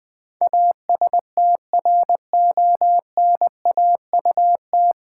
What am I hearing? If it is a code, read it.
Morse code, heard as a keyed tone, about 20 wpm, 700 Hz. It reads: ASTRONAUT
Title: Pangram Challenge v2 - 20wpm